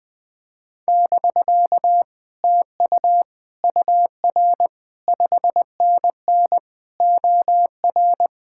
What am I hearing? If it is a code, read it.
Morse code, heard as a keyed tone, about 20 wpm, 700 Hz. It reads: <BK> TU UR 5NN OR
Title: Speed-Racing Common QSO Phrases Encore - 17wpm